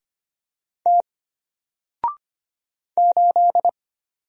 Morse code, heard as a keyed tone, about 25 wpm, 700 Hz. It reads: T 8
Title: Single Letter-Number Alt 25wpm